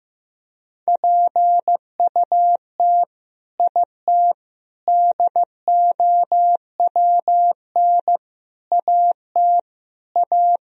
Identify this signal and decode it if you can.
Morse code, heard as a keyed tone, about 15 wpm, 700 Hz. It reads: PUT IT DOWN AT A
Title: Sentences from Top 1000 Words - Review Part 3 - 15wpm